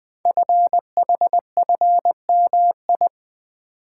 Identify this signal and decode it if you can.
Morse code, heard as a keyed tone, about 20 wpm, 700 Hz. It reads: FHFMI
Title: Code Groups - 5 Letter:Numbers - 20wpm